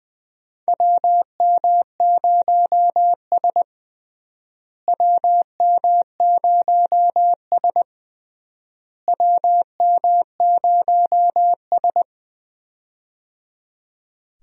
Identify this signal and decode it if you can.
Morse code, heard as a keyed tone, about 20 wpm, 700 Hz. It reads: WM0S WM0S WM0S
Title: USA Call Signs - 4 Characters - Repeated 3x - 20wpm